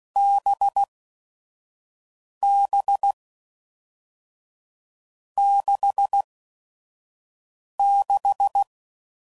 Morse code, heard as a keyed tone, about 16 wpm, spaced as for 5 wpm, 800 Hz. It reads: BB 66
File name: Combinations / Difficult Combos / NDB NDB6